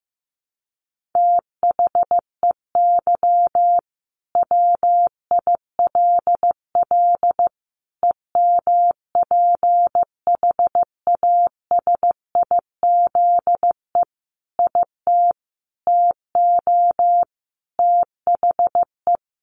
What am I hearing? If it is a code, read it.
Morse code, heard as a keyed tone, about 15 wpm, 700 Hz. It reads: THEY WILL EMPHASIZE IT TO THE
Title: Sentences from Top 1500 Words 15wpm